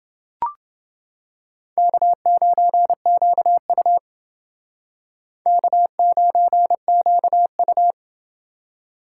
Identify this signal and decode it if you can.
Morse code, heard as a keyed tone, about 29 wpm, 700 Hz. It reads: K9QU K9QU